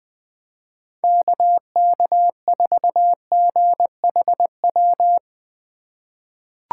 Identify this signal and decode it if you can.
Morse code, heard as a keyed tone, about 20 wpm, 700 Hz. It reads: KK4GHW
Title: USA Call Signs - 6 Characters - 2x Character Spacing - 20wpm